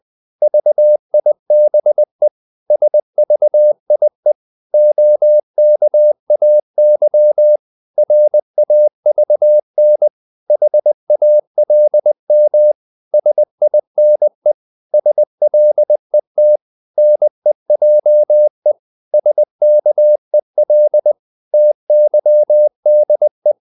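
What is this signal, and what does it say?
Morse code, heard as a keyed tone, about 20 wpm, 600 Hz. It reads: VIBE SVIE OKAY RAVN HALM SINE SLET NEJE SKEL TYDE
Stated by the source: Morse code practice of short Danish words at 20 wpm, from CW med Gnister.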